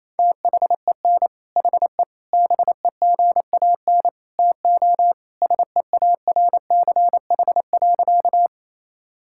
Morse code, heard as a keyed tone, about 28 wpm, 700 Hz. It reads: THEN HE BEGAN TO SEARCH.
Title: Sentences from Top 1000 Words - Review Part 5 - 28wpm